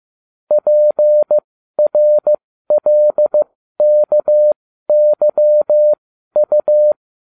Morse code, heard as a keyed tone, about 15 wpm, spaced as for 12 wpm, 600 Hz. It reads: PRLKYU
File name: cw / ubacw20 3